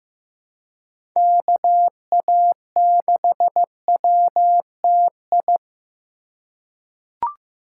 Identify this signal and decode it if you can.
Morse code, heard as a keyed tone, about 15 wpm, 700 Hz. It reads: KA6WTI
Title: USA Call Signs - 6 Characters - 2x Character Spacing - 15wpm